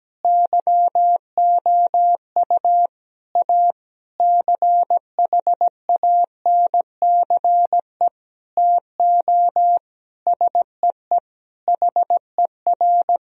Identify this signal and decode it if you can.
Morse code, heard as a keyed tone, about 17 wpm, 700 Hz. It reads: YOU A CHANCE TO SEE HER
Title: Sentences from Top 1000 Words - Review Part 4 - 17wpm